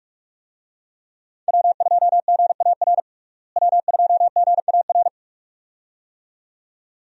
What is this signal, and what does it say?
Morse code, heard as a keyed tone, about 46 wpm, 700 Hz. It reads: W2GAR W2GAR
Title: USA Call Signs - 5 Characters - Repeated 2x - 45wpm